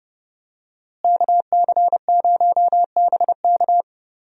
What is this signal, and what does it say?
Morse code, heard as a keyed tone, about 30 wpm, 700 Hz. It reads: KC0BK
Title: USA Call Signs - 5 Characters - Repeated 3x - 30wpm